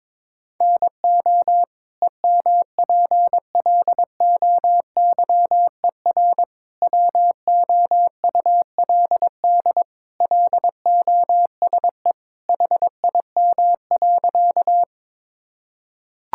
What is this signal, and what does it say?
Morse code, heard as a keyed tone, about 22 wpm, 700 Hz. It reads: NO EMPLOYER WOULD LOSE HIM.
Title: Sentences from Top 1600 Words 22wpm